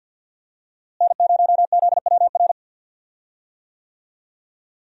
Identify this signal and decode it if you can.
Morse code, heard as a keyed tone, about 47 wpm, 700 Hz. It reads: N0ZWR
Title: USA Call Signs - 5 Characters - Repeated 3x - 50wpm